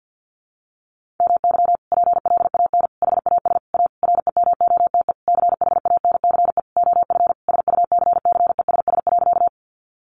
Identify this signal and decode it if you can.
Morse code, heard as a keyed tone, about 50 wpm, 700 Hz. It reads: MY PLAN HAS A REMOTE CHANCE OF SUCCESS.
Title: Sentences from Top 1300 Words 50wpm